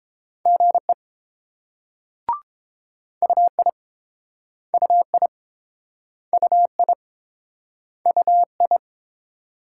Morse code, heard as a keyed tone, about 30 wpm, 700 Hz. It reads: GE UI UI UI UI